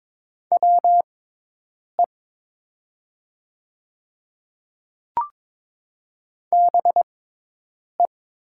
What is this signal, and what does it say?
Morse code, heard as a keyed tone, about 22 wpm, spaced as for 8 wpm, 700 Hz. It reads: WE BE
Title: 6x Character Spacing - Top 100 Words - 22wpm